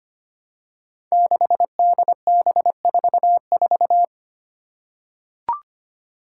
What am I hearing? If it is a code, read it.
Morse code, heard as a keyed tone, about 25 wpm, 700 Hz. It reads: 6DB44
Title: Five Challenging Letter-Number Combinations 25wpm